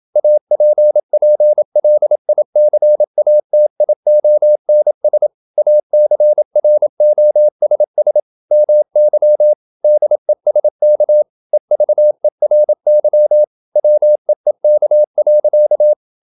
Morse code, heard as a keyed tone, about 28 wpm, 600 Hz. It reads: APPLICATIONS ACROSS MY DESK EVERY WEEK.
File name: Never 0027wpm